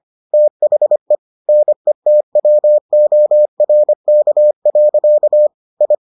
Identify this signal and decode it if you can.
Morse code, heard as a keyed tone, about 25 wpm, 600 Hz. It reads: THE NETWORK. I